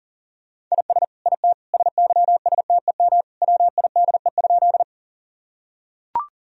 Morse code, heard as a keyed tone, about 40 wpm, 700 Hz. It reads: IS IT SYSTEM WIDE?
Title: Sentences from Top 1000 Words - Review Part 3 - 40wpm